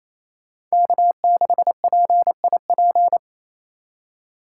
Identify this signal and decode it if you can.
Morse code, heard as a keyed tone, about 33 wpm, 700 Hz. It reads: K6PIP
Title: Speed Racing - USA 1x3 Call Signs 28wpm